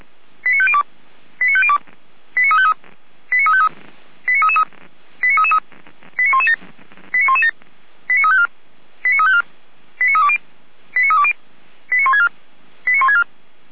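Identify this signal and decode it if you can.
Five-tone selective call (ZVEI), heard as quick runs of five tones, tones of about 70 ms. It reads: 89502 89502 89353 89353 89333 89333 89117 89117 89354 89354 89320 89320 89164 89164
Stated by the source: Five-tone ZVEI selective call received by SDR, from audio.f85360.feuerwehr-orlamuende.org (Feuerwehr Orlamünde fire brigade).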